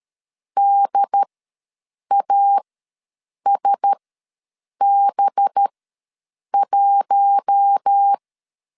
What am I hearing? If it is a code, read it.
Morse code, heard as a keyed tone, about 13 wpm, spaced as for 7 wpm, 800 Hz. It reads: DASB1
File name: LEZIONI / 7 GF9Z / 7 GF9Z ws 12wpm 3321